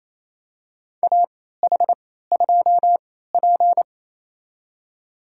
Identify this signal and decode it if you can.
Morse code, heard as a keyed tone, about 28 wpm, spaced as for 16 wpm, 700 Hz. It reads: AH2P
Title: USA Call Signs - 4 Characters - 3x Character Spacing - 28wpm